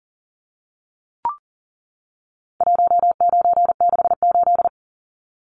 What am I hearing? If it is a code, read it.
Morse code, heard as a keyed tone, about 39 wpm, 700 Hz. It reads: 1968
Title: Numbers - 4-Digit 40wpm